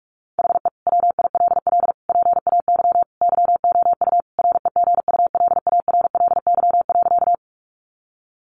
Morse code, heard as a keyed tone, about 44 wpm, 700 Hz. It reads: HE WILL PAY YOU REGULARLY.
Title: Sentences from Top 1400 Words 45wpm